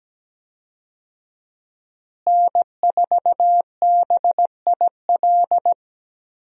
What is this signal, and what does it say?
Morse code, heard as a keyed tone, about 17 wpm, 700 Hz. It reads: N4BIL
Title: USA Call Signs - 5 Characters - Repeated 2x - 17wpm